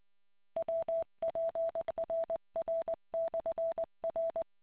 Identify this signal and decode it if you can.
Morse code, heard as a keyed tone, about 20 wpm, 660 Hz. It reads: WPRR/R